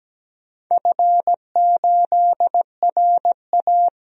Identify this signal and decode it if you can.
Morse code, heard as a keyed tone, about 17 wpm, 700 Hz. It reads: F8RA
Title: DXCC-100 Call Signs - 4 Characters - Repeated 3x - 17wpm